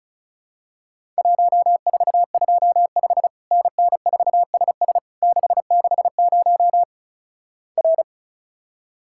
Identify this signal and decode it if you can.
Morse code, heard as a keyed tone, about 36 wpm, 700 Hz. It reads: 1425 NN4SS 660 R
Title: Worked All Europe DX Contest - QTC Exchange - 35wpm